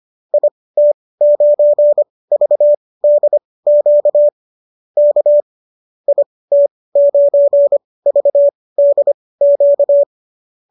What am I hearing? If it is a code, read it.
Morse code, heard as a keyed tone, about 25 wpm, spaced as for 18 wpm, 590 Hz. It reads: IT9VDQ K IT9VDQ
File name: qso215 18